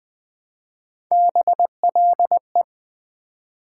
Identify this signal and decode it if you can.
Morse code, heard as a keyed tone, about 20 wpm, 700 Hz. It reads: BLE
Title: Build A Word - 1 Letter At A Time - 3 to 5-Letter Words - 20wpm